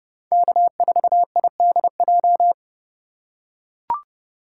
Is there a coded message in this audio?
Yes: Morse code from 0.3 to 2.5 s, heard as a keyed tone, about 29 wpm, 700 Hz. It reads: K4IDJ